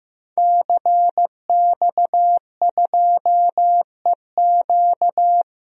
Morse code, heard as a keyed tone, about 15 wpm, 700 Hz. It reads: CX2EQ
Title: DXCC-100 Call Signs - 5 Characters - 15wpm